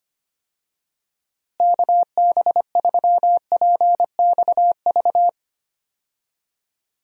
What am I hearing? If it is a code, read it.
Morse code, heard as a keyed tone, about 25 wpm, 700 Hz. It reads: KB3PXV
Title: USA Call Signs - 6 Characters - Repeated 3x - 25wpm